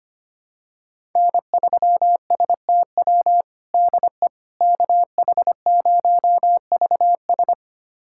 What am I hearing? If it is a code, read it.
Morse code, heard as a keyed tone, about 25 wpm, 700 Hz. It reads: N3STW DE KH0VS
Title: USA Callsign DE Callsign - 5 Characters - 25wpm